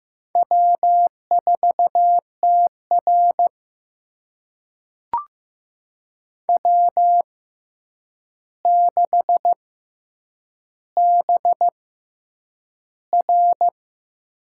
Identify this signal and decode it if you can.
Morse code, heard as a keyed tone, about 15 wpm, spaced as for 8 wpm, 700 Hz. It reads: W4TR W6BR
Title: USA Call Signs - 4 Characters - 6x Character Spacing - 15wpm